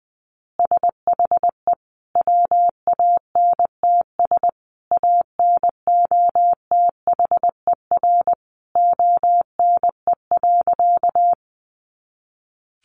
Morse code, heard as a keyed tone, about 20 wpm, 700 Hz. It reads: SHE WANTS ANOTHER ONE.
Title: Easy Phrases 20wpm